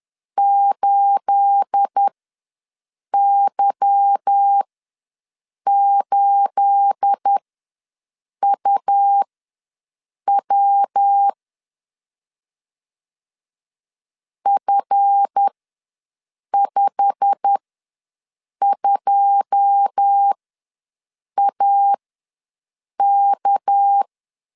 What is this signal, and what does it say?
Morse code, heard as a keyed tone, about 10 wpm, spaced as for 6 wpm, 800 Hz. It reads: 8Y8UW F52AK
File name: LEZIONI / 9 28 ws 10wpm 4124